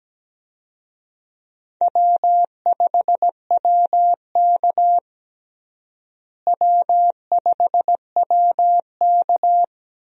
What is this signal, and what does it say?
Morse code, heard as a keyed tone, about 17 wpm, 700 Hz. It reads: W5WK W5WK
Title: USA Call Signs - 4 Characters - Repeated 2x - 17wpm